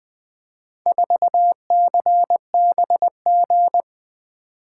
Morse code, heard as a keyed tone, about 20 wpm, 700 Hz. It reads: 4CBG
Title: Four Letter-Number Combinations - 20wpm